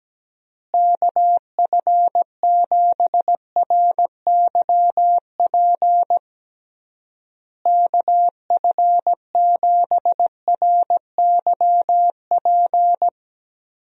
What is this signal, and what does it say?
Morse code, heard as a keyed tone, about 17 wpm, 700 Hz. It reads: KF7RYP KF7RYP
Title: USA Call Signs - 6 Characters - Repeated 3x - 17wpm